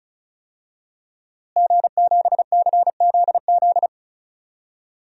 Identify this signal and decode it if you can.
Morse code, heard as a keyed tone, about 36 wpm, 700 Hz. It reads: G7CZZ
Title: DXCC-100 Call Signs - 5 Characters - Repeated 2x - 35wpm